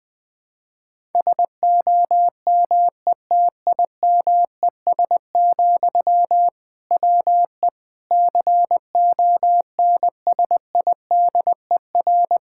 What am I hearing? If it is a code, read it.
Morse code, heard as a keyed tone, about 20 wpm, 700 Hz. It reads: SOMETIMES, WE CONSIDER
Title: Sentences from Top 1000 Words - Review Part 5 - 20wpm